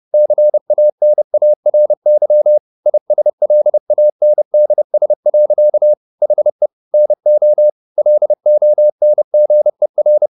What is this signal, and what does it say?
Morse code, heard as a keyed tone, about 30 wpm, 600 Hz. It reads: CANARY ISLANDS. HE NO LONGER
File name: the old man and the sea / the old man and the sea 1 30wpm